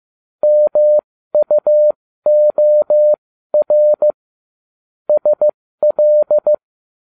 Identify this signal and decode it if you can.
Morse code, heard as a keyed tone, about 15 wpm, spaced as for 13 wpm, 600 Hz. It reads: MUOR SL